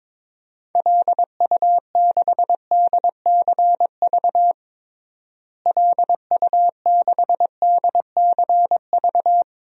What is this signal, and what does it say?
Morse code, heard as a keyed tone, about 22 wpm, 700 Hz. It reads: LU6DCV LU6DCV
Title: DXCC-100 Call Signs - 6 Characters - Repeated 2x - 22wpm